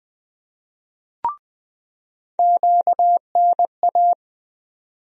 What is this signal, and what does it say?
Morse code, heard as a keyed tone, about 20 wpm, 700 Hz. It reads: QNA
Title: USA Call Sign Postfixes - 3 Letters 20wpm